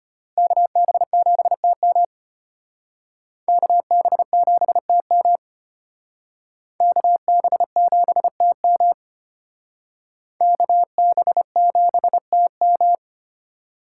Morse code, heard as a keyed tone, about 31 wpm, 700 Hz. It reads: KB7TM KB7TM KB7TM KB7TM
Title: Speed Racing - USA Call Signs 25wpm